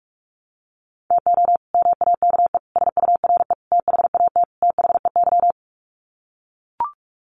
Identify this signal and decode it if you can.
Morse code, heard as a keyed tone, about 44 wpm, 700 Hz. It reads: TO MAKE SURE THAT THEY
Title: Sets of 5 Words Encore2 45wpm